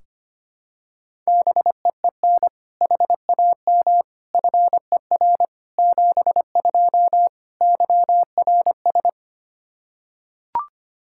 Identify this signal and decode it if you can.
Morse code, heard as a keyed tone, about 25 wpm, 700 Hz. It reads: BEEN HAM FER 72 YRS
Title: Practice QSO - Exchange 3 - More Info - 25wpm